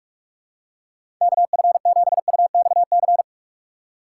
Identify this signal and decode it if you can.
Morse code, heard as a keyed tone, about 46 wpm, 700 Hz. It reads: KF7UXC
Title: K1USN SST QSO Practice - 45wpm